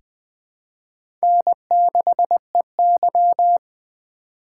Morse code, heard as a keyed tone, about 20 wpm, 700 Hz. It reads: N6EY
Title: USA Call Signs - 4x Character Spacing - 20wpm